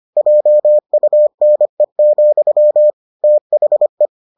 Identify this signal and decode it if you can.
Morse code, heard as a keyed tone, about 25 wpm, 600 Hz. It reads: JUNE, THE